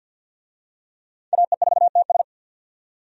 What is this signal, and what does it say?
Morse code, heard as a keyed tone, about 50 wpm, 700 Hz. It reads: AE4TS
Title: USA Call Signs - 5 Characters - 50wpm